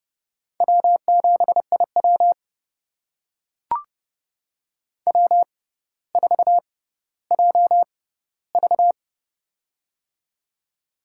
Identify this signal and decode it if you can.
Morse code, heard as a keyed tone, about 31 wpm, spaced as for 15 wpm, 700 Hz. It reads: W7IW W4JV
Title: USA Call Signs - 4 Characters - 6x Character Spacing - 30wpm